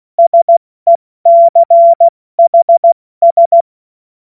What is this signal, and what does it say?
Morse code, heard as a keyed tone, about 16 wpm, 680 Hz. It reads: SECHS
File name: Morsekurs / Witze / Witz 29. November 2022 70 F80